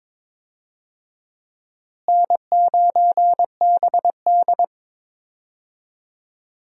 Morse code, heard as a keyed tone, about 22 wpm, 700 Hz. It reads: N9BD
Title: USA Call Signs - 4 Characters - Repeated 2x - 22wpm